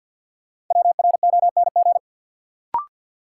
Morse code, heard as a keyed tone, about 51 wpm, 700 Hz. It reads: WRONG
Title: Sets of 5 Words Encore - New Word Drill - 50wpm